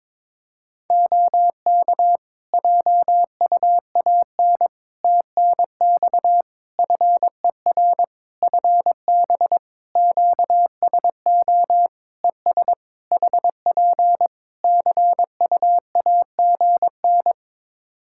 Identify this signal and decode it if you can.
Morse code, heard as a keyed tone, about 22 wpm, 700 Hz. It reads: OK JUAN TNX FER FB QSO ES HP CUAGN
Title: Practice QSO - Exchange 4 - Ending - 22wpm